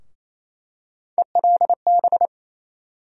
Morse code, heard as a keyed tone, about 28 wpm, 700 Hz. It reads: ELB
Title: USA Call Sign Postfixes - 3 Letters 28wpm